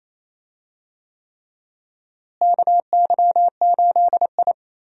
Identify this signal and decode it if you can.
Morse code, heard as a keyed tone, about 27 wpm, 700 Hz. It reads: KY8I
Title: USA Call Signs - 4 Characters - Repeated 3x - 28wpm